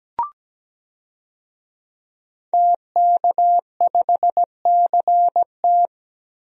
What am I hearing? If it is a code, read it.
Morse code, heard as a keyed tone, about 17 wpm, 700 Hz. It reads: TK5CT
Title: DXCC-100 Call Signs - 5 Characters - Repeated 2x - 17wpm